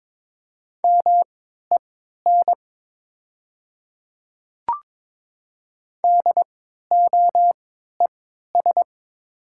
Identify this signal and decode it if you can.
Morse code, heard as a keyed tone, about 22 wpm, spaced as for 13 wpm, 700 Hz. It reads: MEN DOES
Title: 3x Character Spacing - Top 200 Words - 22wpm